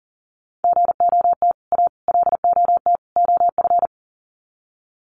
Morse code, heard as a keyed tone, about 40 wpm, 700 Hz. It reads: GOT A LOT OF
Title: Sets of 4 Words 40wpm